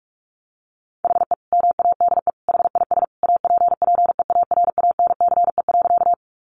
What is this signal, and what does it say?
Morse code, heard as a keyed tone, about 44 wpm, 700 Hz. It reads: HE MADE HIS APPEARANCE.
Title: Sentences from Top 1100 Words 45wpm